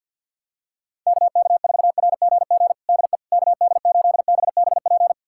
Morse code, heard as a keyed tone, about 47 wpm, 700 Hz. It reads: KK4RGG DE KD8BBP
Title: USA Callsign DE Callsign 50wpm